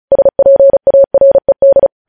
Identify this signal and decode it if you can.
Morse code, heard as a keyed tone, about 36 wpm, 550 Hz. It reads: SPARED
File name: podcast / News headlines-1x-35x35